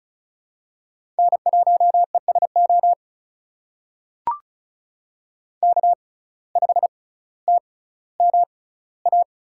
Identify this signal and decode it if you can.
Morse code, heard as a keyed tone, about 36 wpm, spaced as for 18 wpm, 700 Hz. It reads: N1ESO K5TMA
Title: USA Call Signs - 5 Characters - 6x Character Spacing - 35wpm